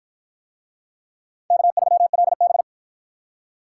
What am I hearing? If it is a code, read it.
Morse code, heard as a keyed tone, about 48 wpm, 700 Hz. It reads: K3LB